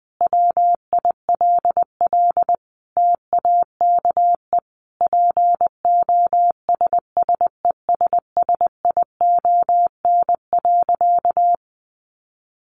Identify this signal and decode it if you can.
Morse code, heard as a keyed tone, about 20 wpm, 700 Hz. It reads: WILL TAKE POSSESSION.
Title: Sentences from Top 1400 Words 20wpm